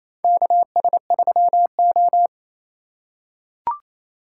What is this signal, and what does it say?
Morse code, heard as a keyed tone, about 28 wpm, 700 Hz. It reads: KS3O
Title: USA Call Signs - 4x Character Spacing - 28wpm